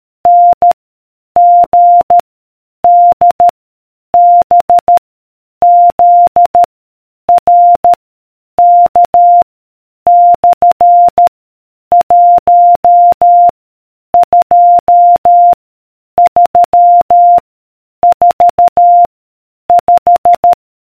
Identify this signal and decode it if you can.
Morse code, heard as a keyed tone, about 13 wpm, 700 Hz. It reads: N G D B Z R K / 1 2 3 4 5